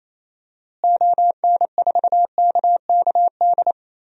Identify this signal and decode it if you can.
Morse code, heard as a keyed tone, about 28 wpm, 700 Hz. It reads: ON4KKD
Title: DXCC-100 Call Signs - 6 Characters - Repeated 2x - 28wpm